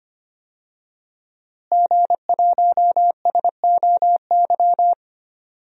Morse code, heard as a keyed tone, about 25 wpm, 700 Hz. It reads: G1SOY
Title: DXCC-100 Call Signs - 5 Characters - Repeated 2x - 25wpm